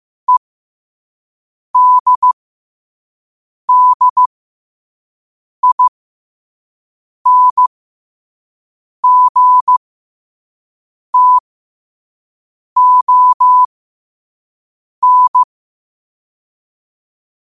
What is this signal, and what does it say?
Morse code, heard as a keyed tone, about 15 wpm, spaced as for 5 wpm, 1000 Hz. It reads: EDDINGTON
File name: 5wpm